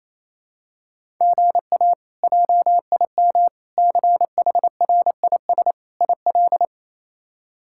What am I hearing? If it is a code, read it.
Morse code, heard as a keyed tone, about 28 wpm, 700 Hz. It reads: GA JIM CHRIS IL
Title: K1USN SST QSO Practice - 28wpm